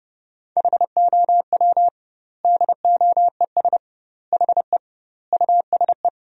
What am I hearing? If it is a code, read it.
Morse code, heard as a keyed tone, about 32 wpm, 700 Hz. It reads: HOW DOES HE USE
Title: 2x Word-Spacing - Sentences from Top 200 Words - 30wpm